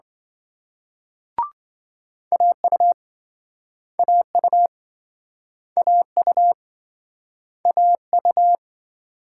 Morse code, heard as a keyed tone, about 28 wpm, 700 Hz. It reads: AU AU AU AU